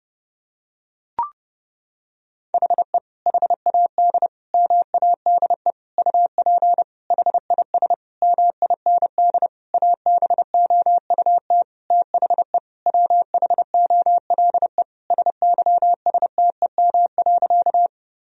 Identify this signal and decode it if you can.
Morse code, heard as a keyed tone, about 30 wpm, 700 Hz. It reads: HE HAD MADE UP HIS MIND ABOUT THE WHOLE SYSTEM.
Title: Sentences from Top 1000 Words - Review Part 3 - 30wpm